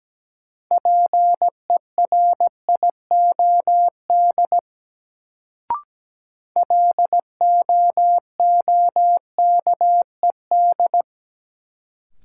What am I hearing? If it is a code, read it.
Morse code, heard as a keyed tone, about 17 wpm, 700 Hz. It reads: PERIOD LOOKED
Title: Sentences from Top 800 Words - New Word Drill - 17wpm